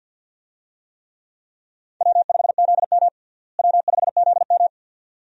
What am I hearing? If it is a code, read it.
Morse code, heard as a keyed tone, about 49 wpm, 700 Hz. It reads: W5ZM W5ZM
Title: USA Call Signs - 4 Characters - Repeated 2x - 50wpm